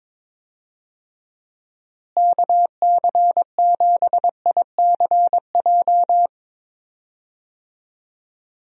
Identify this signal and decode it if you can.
Morse code, heard as a keyed tone, about 21 wpm, 700 Hz. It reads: KC7ICJ
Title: USA Call Signs - 6 Characters - Repeated 3x - 22wpm